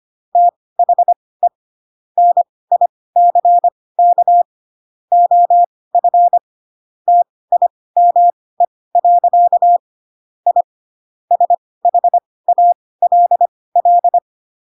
Morse code, heard as a keyed tone, about 25 wpm, spaced as for 18 wpm, 700 Hz. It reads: THE NICK OF TIME. I SHALL
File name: Pinocchio / Pinocchio ch1 12 25 ik1hge